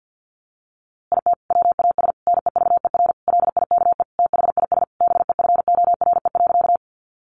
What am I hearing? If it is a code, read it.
Morse code, heard as a keyed tone, about 50 wpm, 700 Hz. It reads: IT WAS NEVER LIKE THIS BEFORE.
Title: A Sound Of Thunder 50wpm